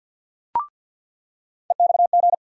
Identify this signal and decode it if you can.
Morse code, heard as a keyed tone, about 47 wpm, 700 Hz. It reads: EXG
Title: USA Call Sign Postfixes - 3 Letters 50wpm